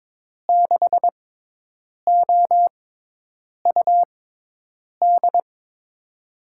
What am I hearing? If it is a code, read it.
Morse code, heard as a keyed tone, about 21 wpm, spaced as for 8 wpm, 700 Hz. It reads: 6OUD
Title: USA Call Signs - 6 Characters - 6x Character Spacing - 22wpm